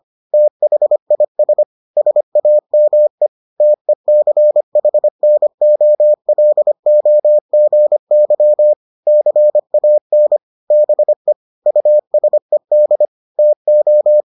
Morse code, heard as a keyed tone, about 25 wpm, 600 Hz. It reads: THIS SAME TECHNOLOGY CAN BE USED TO